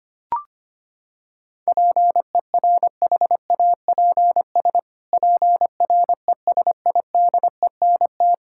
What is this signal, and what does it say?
Morse code, heard as a keyed tone, about 25 wpm, 700 Hz. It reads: PERHAPS PRESIDENT
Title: Pangram Challenge v2 - 25wpm